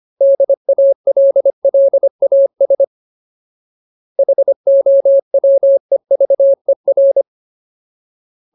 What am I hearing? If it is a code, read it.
Morse code, heard as a keyed tone, about 25 wpm, 550 Hz. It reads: DALLAS HOWEVER
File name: Ultimate Word List / 25WPM Ultimate